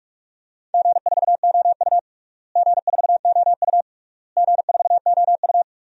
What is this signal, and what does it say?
Morse code, heard as a keyed tone, about 41 wpm, 700 Hz. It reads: G4OU G4OU G4OU
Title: DXCC-100 Call Signs - 4 Characters - Repeated 3x - 45wpm